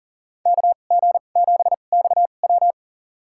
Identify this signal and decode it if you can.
Morse code, heard as a keyed tone, about 40 wpm, spaced as for 29 wpm, 700 Hz. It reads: KG7XW
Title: USA Call Signs - 5 Characters - 2x Character Spacing - 40wpm